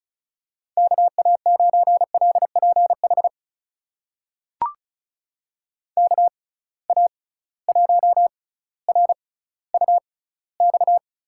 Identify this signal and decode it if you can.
Morse code, heard as a keyed tone, about 36 wpm, spaced as for 18 wpm, 700 Hz. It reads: KA9LPH KA1RUX
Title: USA Call Signs - 6 Characters - 6x Character Spacing - 35wpm